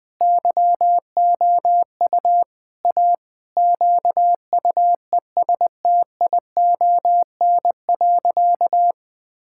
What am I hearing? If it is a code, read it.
Morse code, heard as a keyed tone, about 21 wpm, 700 Hz. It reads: YOU A QUESTION.
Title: Sentences with 5-Word Phrases - Spoken Beginning - 20wpm